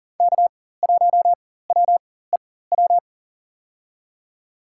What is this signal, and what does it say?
Morse code, heard as a keyed tone, about 40 wpm, spaced as for 19 wpm, 700 Hz. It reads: K1WEW
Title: USA Call Signs - 4x Character Spacing - 40wpm